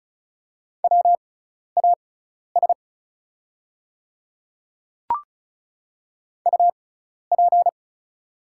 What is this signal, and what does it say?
Morse code, heard as a keyed tone, about 36 wpm, spaced as for 12 wpm, 700 Hz. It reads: WAS UP
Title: 6x Character Spacing - Top 100 Words - 35wpm